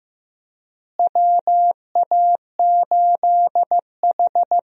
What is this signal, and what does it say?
Morse code, heard as a keyed tone, about 15 wpm, 700 Hz. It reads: WA8H
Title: USA Call Signs - 4 Characters - Repeated 3x - 15wpm